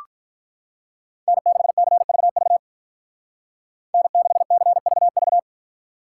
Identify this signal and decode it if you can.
Morse code, heard as a keyed tone, about 46 wpm, 700 Hz. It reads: N6CVV N6CVV